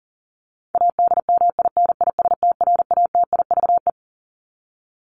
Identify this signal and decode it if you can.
Morse code, heard as a keyed tone, about 37 wpm, 700 Hz. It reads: ADMINISTRATIVE
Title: Top 1501-1750 Words 40wpm